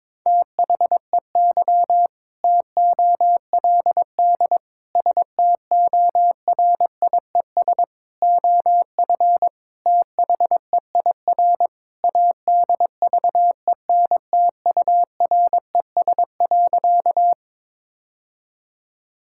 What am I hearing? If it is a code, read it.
Morse code, heard as a keyed tone, about 22 wpm, 700 Hz. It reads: THEY TOLD STORIES OF THEIR ADVENTURES.
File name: practice / 22wpm / Sentences with 2-Word Phrases - 22wpm